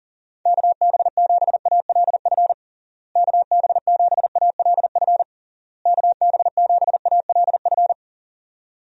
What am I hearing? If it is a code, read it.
Morse code, heard as a keyed tone, about 37 wpm, 700 Hz. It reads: KB7ALF KB7ALF KB7ALF
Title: USA Call Signs - 6 Characters - Repeated 3x - 40wpm